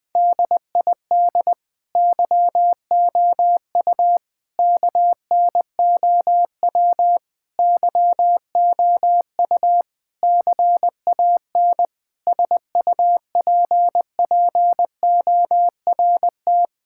Morse code, heard as a keyed tone, about 20 wpm, 700 Hz. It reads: DID YOU KNOW YOU CAN SUPPORT
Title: Sentences with 9-Word Phrases - 20wpm